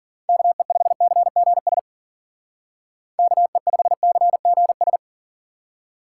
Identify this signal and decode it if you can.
Morse code, heard as a keyed tone, about 52 wpm, 700 Hz. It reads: KE5CGS KE5CGS